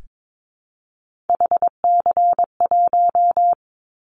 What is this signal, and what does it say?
Morse code, heard as a keyed tone, about 22 wpm, 700 Hz. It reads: HC1
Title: Callsign Prefixes for Top 100 DXCC - 22wpm